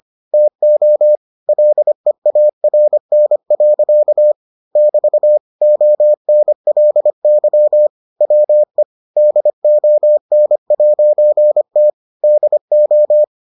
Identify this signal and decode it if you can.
Morse code, heard as a keyed tone, about 25 wpm, 600 Hz. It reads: TO LEARN. <BT> ONLY WE DON'T DO